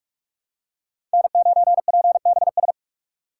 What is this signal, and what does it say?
Morse code, heard as a keyed tone, about 46 wpm, 700 Hz. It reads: N9PBS
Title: USA Call Signs - 3x Character Spacing - 45wpm